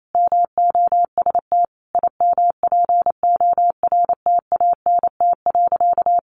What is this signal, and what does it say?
Morse code, heard as a keyed tone, about 28 wpm, 700 Hz. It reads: MOST IMPORTANT.
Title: Sentences from Top 1300 Words 28wpm